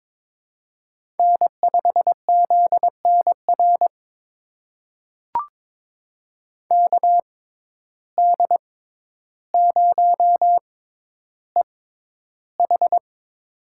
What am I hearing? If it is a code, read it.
Morse code, heard as a keyed tone, about 22 wpm, spaced as for 11 wpm, 700 Hz. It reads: N5ZNR KD0EH
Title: USA Call Signs - 5 Characters - 6x Character Spacing - 22wpm